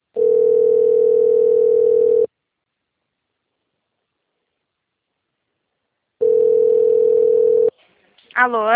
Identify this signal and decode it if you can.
Ringback tone (North America), 440 + 480 Hz, with speech.
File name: rg-1600-unknown- - - .14034